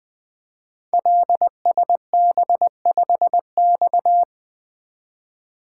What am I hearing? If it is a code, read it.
Morse code, heard as a keyed tone, about 20 wpm, 700 Hz. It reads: LSB5X
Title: Code Groups - 5 Letter:Numbers - 20wpm